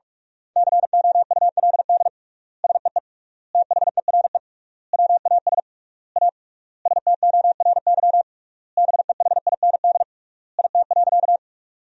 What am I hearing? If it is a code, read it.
Morse code, heard as a keyed tone, about 46 wpm, 700 Hz. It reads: COULD SEE THERE WAS A STORY BEHIND IT.
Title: 3x Word-Spacing - Sentences from Top 500 Words - 45wpm